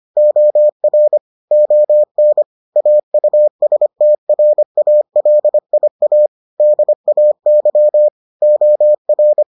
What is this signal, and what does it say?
Morse code, heard as a keyed tone, about 25 wpm, 600 Hz. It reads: OR ON AUSTRALIA DAY OR